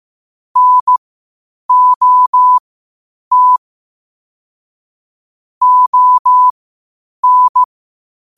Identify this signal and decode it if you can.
Morse code, heard as a keyed tone, about 15 wpm, spaced as for 8 wpm, 1000 Hz. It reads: NOT ON
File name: qotdfiles / 8wpm